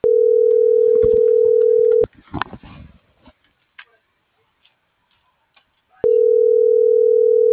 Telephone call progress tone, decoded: ringback tone

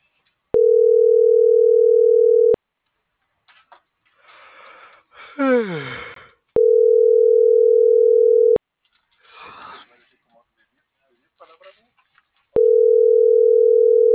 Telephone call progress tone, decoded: ringback tone

